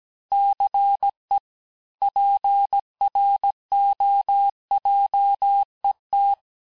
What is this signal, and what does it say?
Morse code, heard as a keyed tone, about 17 wpm, 800 Hz. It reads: CE PROJET
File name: cours-du-lundi files / cours-1587-texte-clair-vitesse-16mpm-puis-21mpm